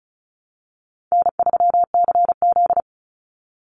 Morse code, heard as a keyed tone, about 34 wpm, 700 Hz. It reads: N3CZ
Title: Easy Call Signs Repeated 3x - 35wpm